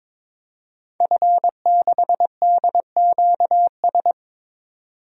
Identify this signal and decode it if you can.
Morse code, heard as a keyed tone, about 21 wpm, 700 Hz. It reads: F6DQS
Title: DXCC-100 Call Signs - 5 Characters - 22wpm